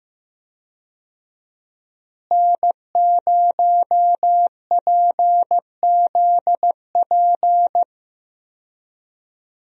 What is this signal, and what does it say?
Morse code, heard as a keyed tone, about 15 wpm, 700 Hz. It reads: N0PZP
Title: USA Call Signs - 5 Characters - Repeated 3x - 15wpm